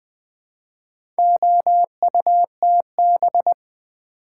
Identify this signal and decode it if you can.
Morse code, heard as a keyed tone, about 20 wpm, 700 Hz. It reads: OUTB
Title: Build A Word - 4 Letters At A Time - 8-Letter Words - 20wpm